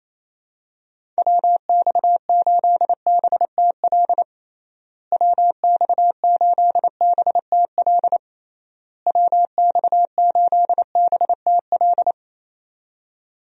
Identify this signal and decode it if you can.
Morse code, heard as a keyed tone, about 27 wpm, 700 Hz. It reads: WX8BTL WX8BTL WX8BTL
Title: USA Call Signs - 6 Characters - Repeated 3x - 28wpm